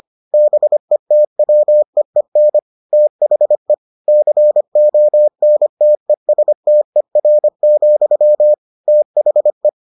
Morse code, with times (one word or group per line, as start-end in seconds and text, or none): 0.3-2.6 BETWEEN
2.9-3.7 THE
4.1-8.5 CONTESTER,
8.9-9.7 THE